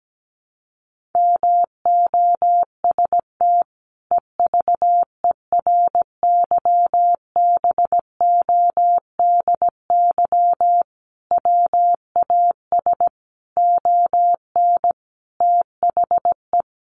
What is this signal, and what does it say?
Morse code, heard as a keyed tone, about 17 wpm, 700 Hz. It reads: MOST EVERYBODY WAS ON THE
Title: Sentences from Top 1400 Words 17wpm